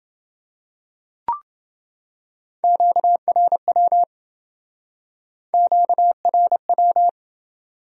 Morse code, heard as a keyed tone, about 27 wpm, 700 Hz. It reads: QRW QRW